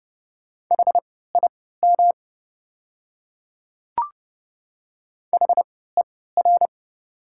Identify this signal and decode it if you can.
Morse code, heard as a keyed tone, about 30 wpm, spaced as for 17 wpm, 700 Hz. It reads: HIM HER
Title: 3x Character Spacing - Top 100 Words - 30wpm